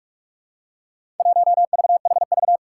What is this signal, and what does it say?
Morse code, heard as a keyed tone, about 44 wpm, 700 Hz. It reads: 1VHV
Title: Four Challenging Letter-Number Combinations 45wpm